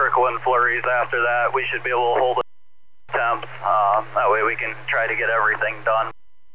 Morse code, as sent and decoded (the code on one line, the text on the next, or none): none